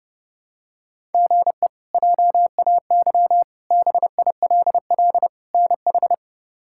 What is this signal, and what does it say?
Morse code, heard as a keyed tone, about 32 wpm, 700 Hz. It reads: GE JAY BILL NH
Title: K1USN SST QSO Practice - 30wpm